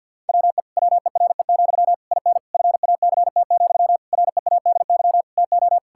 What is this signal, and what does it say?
Morse code, heard as a keyed tone, about 50 wpm, 700 Hz. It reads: WE WERE, IN FACT, READY TO
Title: Sentences from Top 1000 Words - Review Part 5 - 50wpm